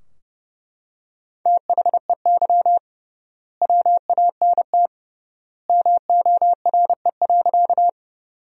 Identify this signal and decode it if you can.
Morse code, heard as a keyed tone, about 30 wpm, 700 Hz. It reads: THEY WANT MORE.
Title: 3x Word-Spacing - Sentences from Top 200 Words - 30wpm